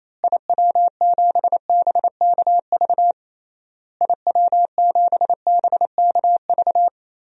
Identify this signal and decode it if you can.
Morse code, heard as a keyed tone, about 28 wpm, 700 Hz. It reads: IW7BKV IW7BKV
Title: DXCC-100 Call Signs - 6 Characters - Repeated 3x - 28wpm